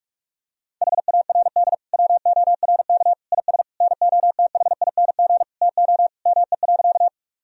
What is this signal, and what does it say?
Morse code, heard as a keyed tone, about 46 wpm, 700 Hz. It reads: HARD WORK IS NOTHING TO ME.
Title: Sentences from Top 1000 Words - Review Part 1 - 45wpm